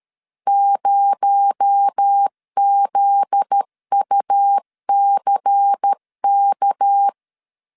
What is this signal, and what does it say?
Morse code, heard as a keyed tone, about 13 wpm, 800 Hz. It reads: 0ZUCK
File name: LEZIONI / 7 GF9Z / 7 GF9Z 12wpm 2147